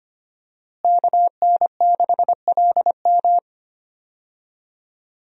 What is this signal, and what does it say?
Morse code, heard as a keyed tone, about 25 wpm, 700 Hz. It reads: KN6LM
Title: USA Call Signs - 5 Characters - Repeated 3x - 25wpm